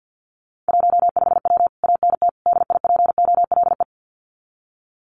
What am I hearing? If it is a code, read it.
Morse code, heard as a keyed tone, about 48 wpm, 700 Hz. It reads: 15W ANT DIPOLE
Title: Common QSO Phrases Encore 50wpm